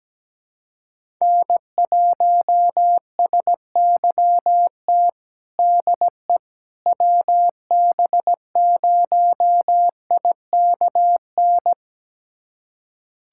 Morse code, heard as a keyed tone, about 17 wpm, 700 Hz. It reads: N1SYT DE WB0IKN
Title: USA Callsign DE Callsign 17wpm